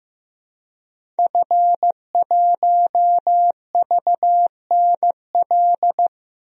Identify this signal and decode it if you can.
Morse code, heard as a keyed tone, about 15 wpm, 700 Hz. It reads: F1VNL
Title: DXCC-100 Call Signs - 5 Characters - 15wpm